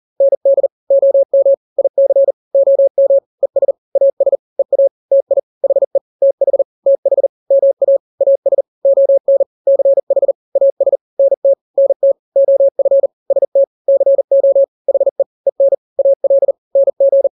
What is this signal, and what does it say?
Morse code, heard as a keyed tone, about 40 wpm, 550 Hz. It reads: ND OM IC OM ES AS EA TI HE TH TH MA AS ON CH AS NT NT OF ST CO HE EN AL NG
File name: n-grams / 2-grams 40 35 WPM